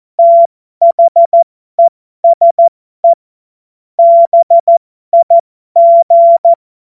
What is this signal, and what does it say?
Morse code, heard as a keyed tone, about 14 wpm, spaced as for 12 wpm, 680 Hz. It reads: THESE BIG
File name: Morsekurs / Witze